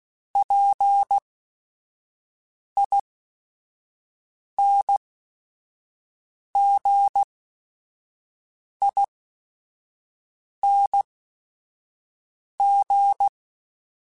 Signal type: Morse code